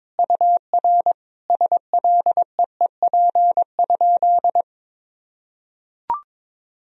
Morse code, heard as a keyed tone, about 21 wpm, 700 Hz. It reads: UR SLEEP?
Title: Sentences from Top 500 Words and Abbreviations 22wpm